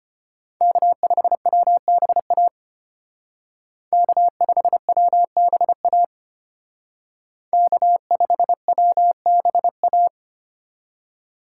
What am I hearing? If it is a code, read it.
Morse code, heard as a keyed tone, about 32 wpm, 700 Hz. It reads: K5WBA K5WBA K5WBA